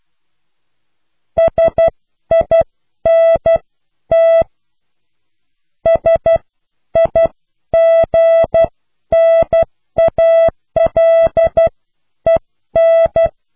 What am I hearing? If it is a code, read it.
Morse code, heard as a keyed tone, about 12 wpm, spaced as for 10 wpm, 660 Hz. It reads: SINT SIGNALEN